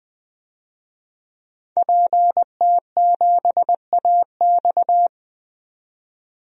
Morse code, heard as a keyed tone, about 20 wpm, 700 Hz. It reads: PT7AX
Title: DXCC-100 Call Signs - 5 Characters - 20wpm